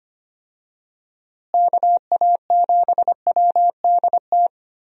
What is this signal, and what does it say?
Morse code, heard as a keyed tone, about 25 wpm, 700 Hz. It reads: KA7WDT
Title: USA Call Signs - 6 Characters - Repeated 3x - 25wpm